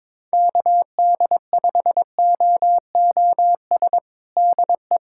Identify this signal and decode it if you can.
Morse code, heard as a keyed tone, about 22 wpm, 700 Hz. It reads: KD5OOS DE
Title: USA Callsign DE Callsign 22wpm